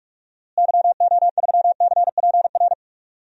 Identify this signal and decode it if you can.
Morse code, heard as a keyed tone, about 46 wpm, 700 Hz. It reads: YO3CPR
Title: DXCC-100 Call Signs - 6 Characters - Repeated 2x - 45wpm